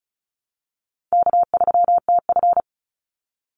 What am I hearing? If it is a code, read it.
Morse code, heard as a keyed tone, about 34 wpm, 700 Hz. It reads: K3TF
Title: Call Signs Repeated 2x - 35wpm